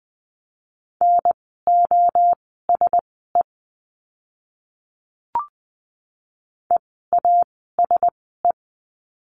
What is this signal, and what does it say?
Morse code, heard as a keyed tone, about 20 wpm, spaced as for 15 wpm, 700 Hz. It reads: NOSE EASE
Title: 036 Lesson 7f - Letter s - Words 20wpm